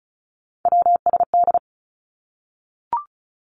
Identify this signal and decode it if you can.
Morse code, heard as a keyed tone, about 36 wpm, 700 Hz. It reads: WSD
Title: Three Challenging Letter-Number Combinations 35wpm